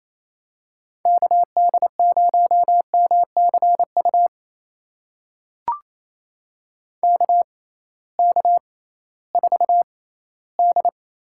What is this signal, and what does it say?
Morse code, heard as a keyed tone, about 28 wpm, spaced as for 16 wpm, 700 Hz. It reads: KD0MCU KK4D
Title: USA Call Signs - 6 Characters - 6x Character Spacing - 28wpm